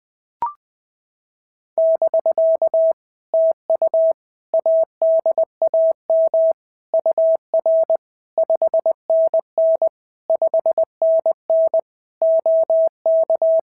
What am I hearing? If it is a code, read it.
Morse code, heard as a keyed tone, about 18 wpm, 650 Hz. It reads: <BK> TU ADAM UR 5NN 5NN OK
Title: POTA QSO Practice - 20wpm